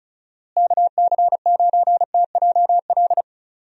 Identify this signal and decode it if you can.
Morse code, heard as a keyed tone, about 36 wpm, 700 Hz. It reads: KC9TJL
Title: USA Call Signs - 6 Characters - Repeated 3x - 35wpm